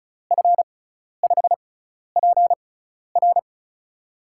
Morse code, heard as a keyed tone, about 36 wpm, spaced as for 12 wpm, 700 Hz. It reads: F5PR
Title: USA Call Signs - 5 Characters - 6x Character Spacing - 35wpm